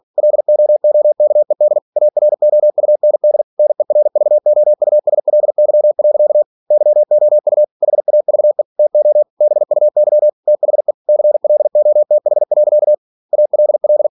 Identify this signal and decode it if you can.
Morse code, heard as a keyed tone, about 46 wpm, 600 Hz. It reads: LOOKED AROUND NERVOUSLY. YOU HAVE TO BUY THE CLOTH. ALL
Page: Never 47wpm